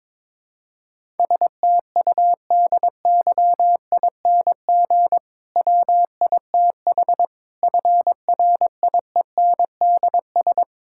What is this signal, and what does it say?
Morse code, heard as a keyed tone, about 23 wpm, 700 Hz. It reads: STUDYING WITH FRIENDS
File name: Difficult Sentences with Hints - 22wpm